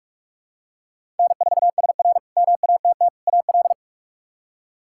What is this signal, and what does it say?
Morse code, heard as a keyed tone, about 46 wpm, 700 Hz. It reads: N4SR MATT AL
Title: K1USN SST QSO Practice - 45wpm